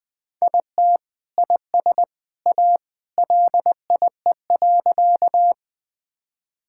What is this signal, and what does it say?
Morse code, heard as a keyed tone, about 20 wpm, 700 Hz. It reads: IT IS A LIE.
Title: Sentences from Top 1600 Words 20wpm